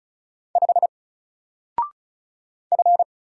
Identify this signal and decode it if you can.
Morse code, heard as a keyed tone, about 34 wpm, 700 Hz. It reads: 5 F
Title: Single Character (Letter:Number:Punctuation:Prosign) 35wpm